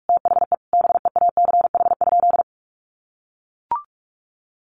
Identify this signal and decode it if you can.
Morse code, heard as a keyed tone, about 44 wpm, 700 Hz. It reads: THE BEACH?
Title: Sentences from Top 1500 Words 45wpm